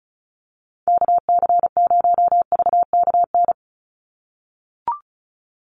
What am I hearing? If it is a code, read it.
Morse code, heard as a keyed tone, about 34 wpm, 700 Hz. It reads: KC0VKN
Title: Call Signs - 35wpm